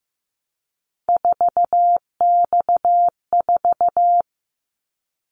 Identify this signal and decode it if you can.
Morse code, heard as a keyed tone, about 15 wpm, 700 Hz. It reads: 4X4
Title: Callsign Prefixes for Top 50 DXCC - 15wpm